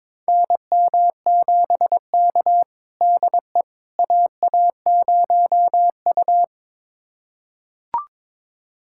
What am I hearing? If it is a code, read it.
Morse code, heard as a keyed tone, about 22 wpm, 700 Hz. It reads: NM7K DE AA0U
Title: USA Callsign DE Callsign - 4 Characters - 22wpm